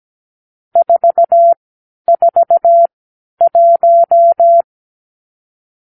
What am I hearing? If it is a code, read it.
Morse code, heard as a keyed tone, about 17 wpm, spaced as for 10 wpm, 680 Hz. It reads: 441
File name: k7qo / Lesson 46 (1-4)